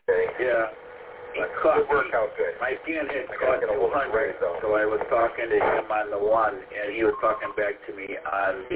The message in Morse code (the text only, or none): none